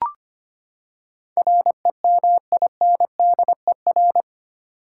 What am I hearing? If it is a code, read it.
Morse code, heard as a keyed tone, about 25 wpm, 700 Hz. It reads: REMINDER
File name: Eight Letter Words - 25wpm